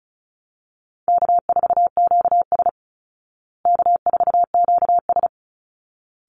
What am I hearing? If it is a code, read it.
Morse code, heard as a keyed tone, about 34 wpm, 700 Hz. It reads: K4QS K4QS
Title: Easy Call Signs Repeated 2x - 35wpm